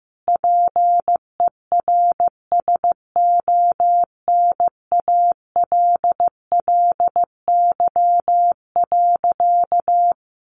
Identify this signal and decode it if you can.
Morse code, heard as a keyed tone, about 15 wpm, 700 Hz. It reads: PERSONALLY.
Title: Sentences from Top 1400 Words 15wpm